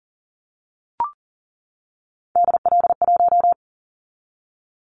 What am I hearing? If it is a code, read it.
Morse code, heard as a keyed tone, about 40 wpm, 700 Hz. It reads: DL1
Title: Callsign Prefixes for Top 25 DXCC - 40wpm